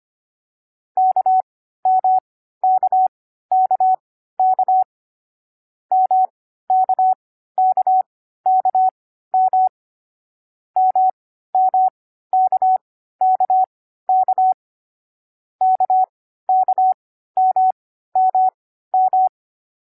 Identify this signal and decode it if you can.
Morse code, heard as a keyed tone, about 25 wpm, spaced as for 14 wpm, 750 Hz. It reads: KMKKK MKKKM MMKKK KKMMM
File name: CW-Koch / koch.2.4.groups5.15wpm